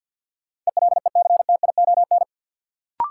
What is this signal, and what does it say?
Morse code, heard as a keyed tone, about 48 wpm, 700 Hz. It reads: ELECTION
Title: Sentences from Top 1500 Words - New Word Drill - 50wpm